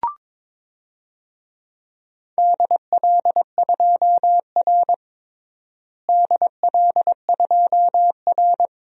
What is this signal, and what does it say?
Morse code, heard as a keyed tone, about 22 wpm, 700 Hz. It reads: DL2R DL2R
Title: DXCC-100 Call Signs - 4 Characters - Repeated 3x - 22wpm